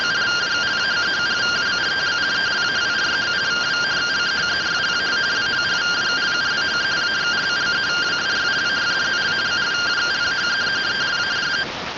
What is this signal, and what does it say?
Two-tone teleprinter signal, about 170 Hz shift.